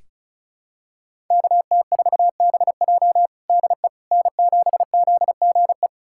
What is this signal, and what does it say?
Morse code, heard as a keyed tone, about 34 wpm, 700 Hz. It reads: KT4BJ DE N7ZGE
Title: USA Callsign DE Callsign - 5 Characters - 35wpm